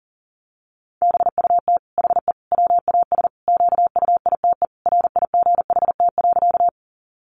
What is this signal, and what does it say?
Morse code, heard as a keyed tone, about 40 wpm, 700 Hz. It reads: BUT HE WAS QUITE RIGHT.
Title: Sentences from Top 700 Words 40wpm